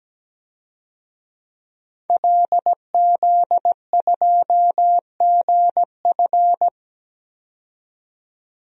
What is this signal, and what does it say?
Morse code, heard as a keyed tone, about 17 wpm, 700 Hz. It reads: LZ2GF
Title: DXCC-100 Call Signs - 5 Characters - 17wpm